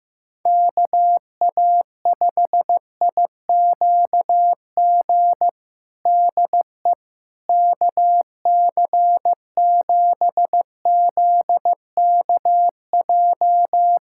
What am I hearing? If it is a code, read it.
Morse code, heard as a keyed tone, about 16 wpm, 700 Hz. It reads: KA5IQG DE KC7ZKJ
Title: USA Callsign DE Callsign - 6 Characters - 15wpm